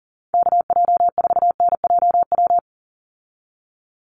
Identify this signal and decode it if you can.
Morse code, heard as a keyed tone, about 40 wpm, 700 Hz. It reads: KJ4NJW
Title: USA Call Signs 40wpm